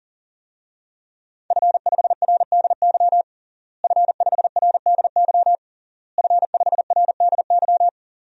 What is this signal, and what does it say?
Morse code, heard as a keyed tone, about 40 wpm, 700 Hz. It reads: F5RDY F5RDY F5RDY
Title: DXCC-100 Call Signs - 5 Characters - Repeated 3x - 40wpm